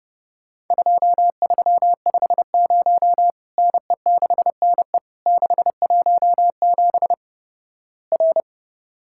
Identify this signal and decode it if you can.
Morse code, heard as a keyed tone, about 29 wpm, 700 Hz. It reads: 2350 NE6NE 617 R
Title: Worked All Europe DX Contest - QTC Exchange - 30wpm